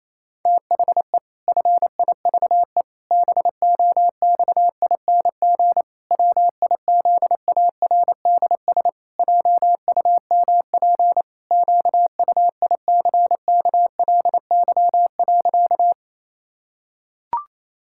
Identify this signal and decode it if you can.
Morse code, heard as a keyed tone, about 28 wpm, 700 Hz. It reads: THE FIVE BOXING WIZARDS JUMP QUICKLY.
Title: Pangram Challenge v2 - 28wpm